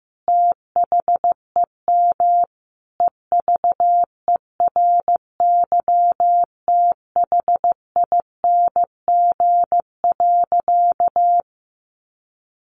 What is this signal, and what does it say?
Morse code, heard as a keyed tone, about 15 wpm, 700 Hz. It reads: THEM EVERYTHING.
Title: Sentences from Top 700 Words 15wpm